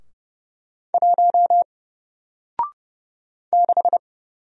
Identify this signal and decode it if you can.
Morse code, heard as a keyed tone, about 30 wpm, 700 Hz. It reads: 1 6
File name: Single-Letter-Number-30wpm